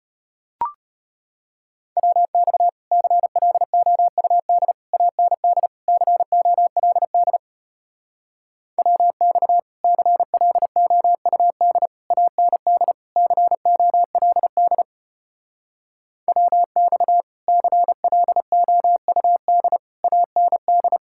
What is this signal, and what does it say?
Morse code, heard as a keyed tone, about 33 wpm, 700 Hz. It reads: WX CLOUD AND COLD WX CLOUD AND COLD WX CLOUD AND